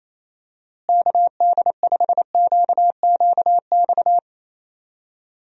Morse code, heard as a keyed tone, about 28 wpm, 700 Hz. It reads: KD5QQX
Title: USA Call Signs - 6 Characters - Repeated 2x - 28wpm